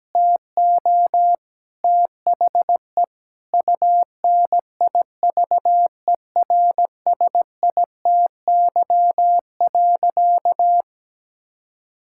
Morse code, heard as a keyed tone, about 17 wpm, 700 Hz. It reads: TO THE UNIVERSITY.
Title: Sentences from Top 1000 Words - Review Part 5 - 17wpm